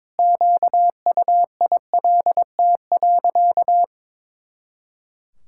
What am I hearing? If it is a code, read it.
Morse code, heard as a keyed tone, about 22 wpm, 700 Hz. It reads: QUILT.
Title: Pangram Challenge v2 - 22wpm